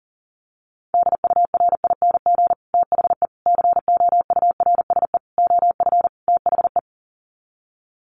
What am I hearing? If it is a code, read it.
Morse code, heard as a keyed tone, about 40 wpm, 700 Hz. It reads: DURING THE COURSE OF THE
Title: Sets of 5 Words Encore 40wpm